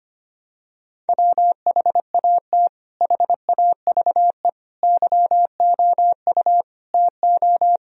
Morse code, heard as a keyed tone, about 25 wpm, 700 Hz. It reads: WHAT HAVE YOU TO
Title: Sentences from Top 1000 Words - Review Part 2 - 25wpm